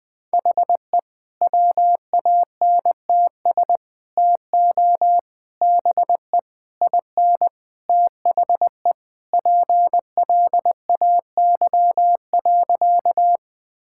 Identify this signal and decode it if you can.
Morse code, heard as a keyed tone, about 20 wpm, 700 Hz. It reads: HE WANTS TO BE IN THE PLAY.
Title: Sentences with 4-Word Phrases - Spoken Beginning - 20wpm